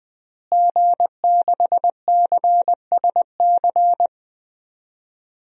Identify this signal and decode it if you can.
Morse code, heard as a keyed tone, about 21 wpm, 700 Hz. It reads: G6CSC
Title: DXCC-100 Call Signs - 5 Characters - Repeated 3x - 20wpm